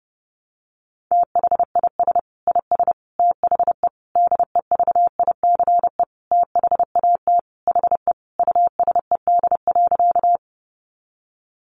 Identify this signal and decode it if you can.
Morse code, heard as a keyed tone, about 30 wpm, 700 Hz. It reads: THIS IS THE DEVICE THAT HE USED.
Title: Sentences from Top 1500 Words 30wpm